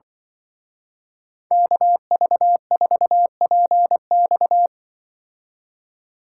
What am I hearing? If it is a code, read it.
Morse code, heard as a keyed tone, about 23 wpm, 700 Hz. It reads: KV4PX
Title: Speed Racing - USA 2x2 Call Signs 20wpm